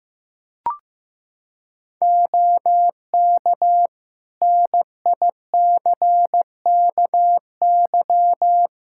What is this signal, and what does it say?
Morse code, heard as a keyed tone, about 15 wpm, 700 Hz. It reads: OK NICKY
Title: Practice QSO - Exchange 2 - Info - 15wpm